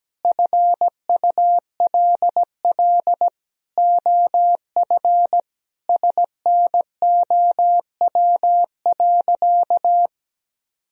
Morse code, heard as a keyed tone, about 17 wpm, 700 Hz. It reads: FULL OF SNOW.
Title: Sentences from Top 1000 Words - Review Part 3 - 17wpm